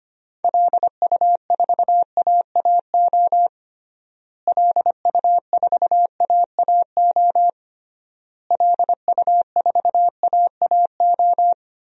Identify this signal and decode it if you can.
Morse code, heard as a keyed tone, about 25 wpm, 700 Hz. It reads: LU4AAO LU4AAO LU4AAO
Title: DXCC-100 Call Signs - 6 Characters - Repeated 3x - 25wpm